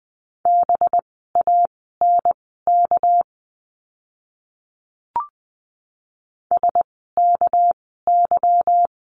Morse code, heard as a keyed tone, about 19 wpm, spaced as for 14 wpm, 700 Hz. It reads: BANK SKY
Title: 199 Lesson 32f - Letter k - Words 20wpm